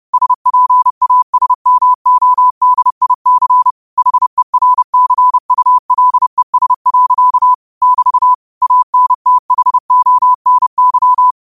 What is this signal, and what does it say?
Morse code, heard as a keyed tone, about 30 wpm, 1000 Hz. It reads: SPASMODIC HERCULES. <BT> ANTHONY
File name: qotdfiles / 30wpm